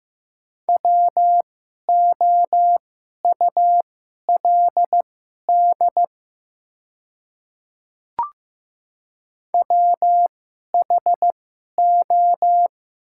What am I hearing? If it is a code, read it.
Morse code, heard as a keyed tone, about 15 wpm, spaced as for 11 wpm, 700 Hz. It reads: WOULD WHO
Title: 2x Character Spacing - Top 200 Words - 15wpm